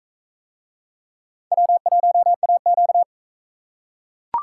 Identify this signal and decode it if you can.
Morse code, heard as a keyed tone, about 40 wpm, 700 Hz. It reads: W1AQ